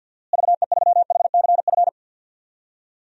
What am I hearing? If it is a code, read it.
Morse code, heard as a keyed tone, about 47 wpm, 700 Hz. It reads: VE3HCF
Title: DXCC-100 Call Signs - 6 Characters - Repeated 3x - 50wpm